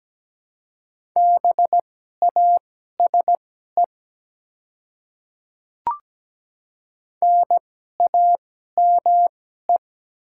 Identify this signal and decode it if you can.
Morse code, heard as a keyed tone, about 17 wpm, spaced as for 12 wpm, 700 Hz. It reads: BASE NAME
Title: 2x Character Spacing - Top 300 Words - 17wpm